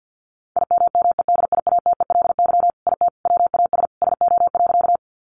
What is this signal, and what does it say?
Morse code, heard as a keyed tone, about 50 wpm, 700 Hz. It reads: IMMEDIATELY IT WAS SO.
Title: Sentences from Top 1100 Words 50wpm